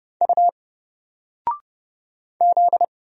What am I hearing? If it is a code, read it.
Morse code, heard as a keyed tone, about 31 wpm, 700 Hz. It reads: U Z
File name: Single-Letter-Number-30wpm